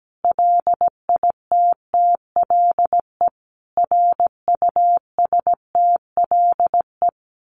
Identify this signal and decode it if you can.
Morse code, heard as a keyed tone, about 17 wpm, 700 Hz. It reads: LITTLE RUSTLE
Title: A Sound Of Thunder 17wpm